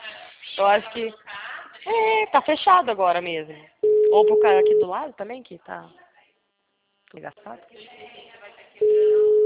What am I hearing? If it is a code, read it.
Ringback tone (Europe, 425 Hz), with speech.
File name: out- -2040- - - .33